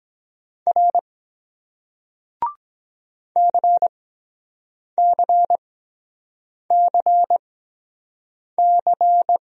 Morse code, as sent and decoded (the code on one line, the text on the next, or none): .-. / -.-. -.-. -.-. -.-.
R CCCC